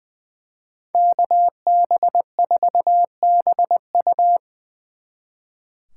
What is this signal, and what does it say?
Morse code, heard as a keyed tone, about 24 wpm, 700 Hz. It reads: KB4BU
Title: Speed Racing - USA 2x2 Call Signs 20wpm